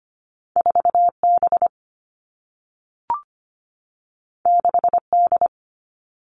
Morse code, heard as a keyed tone, about 25 wpm, 700 Hz. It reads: 4B 6D
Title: All Cut Numbers - 2-Digit 25wpm